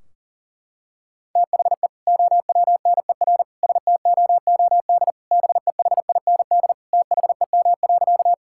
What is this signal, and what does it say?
Morse code, heard as a keyed tone, about 40 wpm, 700 Hz. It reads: THE OWNER STOOD BEHIND THEM.
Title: Sentences from Top 1600 Words 40wpm